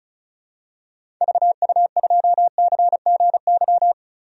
Morse code, heard as a keyed tone, about 34 wpm, 700 Hz. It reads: VU2CGY
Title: DXCC-100 Call Signs - 6 Characters - 35wpm